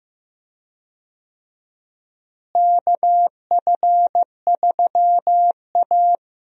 Morse code, heard as a keyed tone, about 15 wpm, 700 Hz. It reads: KF3A
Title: USA Call Signs - 4 Characters - Repeated 2x - 15wpm